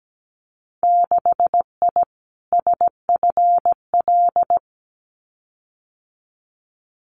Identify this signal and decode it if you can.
Morse code, heard as a keyed tone, about 17 wpm, 700 Hz. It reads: 6I SFL
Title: Winter Field Day Exchanges 17wpm